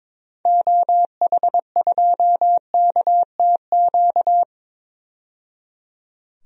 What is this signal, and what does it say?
Morse code, heard as a keyed tone, about 22 wpm, 700 Hz. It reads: OH2KTQ
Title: DXCC-100 Call Signs - 6 Characters - 22wpm